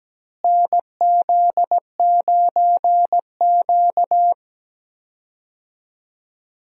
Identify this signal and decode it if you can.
Morse code, heard as a keyed tone, about 17 wpm, 700 Hz. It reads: NZ9Q
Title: USA Call Signs - 4 Characters - Repeated 3x - 17wpm